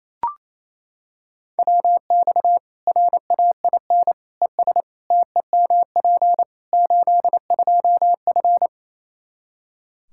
Morse code, heard as a keyed tone, about 28 wpm, 700 Hz. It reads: WX RAIN ES TEMP 82F
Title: Practice QSO - Exchange 2 - Info - 28wpm